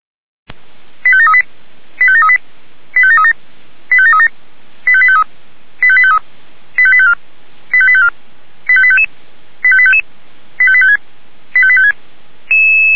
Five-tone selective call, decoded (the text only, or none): 85738 85738 85737 85737 85853 85853 85854 85854 85855 85855 85856 85856